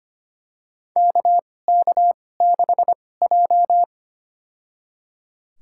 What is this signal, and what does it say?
Morse code, heard as a keyed tone, about 25 wpm, spaced as for 18 wpm, 700 Hz. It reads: KK6J
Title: USA Call Signs - 4 Characters - 2x Character Spacing - 25wpm